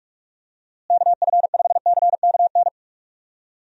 Morse code, heard as a keyed tone, about 46 wpm, 700 Hz. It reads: KF5CKN
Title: USA Call Signs - 6 Characters - Repeated 2x - 45wpm